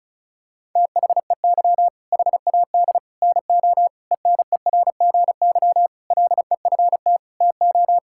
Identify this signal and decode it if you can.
Morse code, heard as a keyed tone, about 36 wpm, 700 Hz. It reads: THEY HAD NO ENERGY LEFT TO
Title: Sentences from Top 1000 Words - Review Part 6 - 35wpm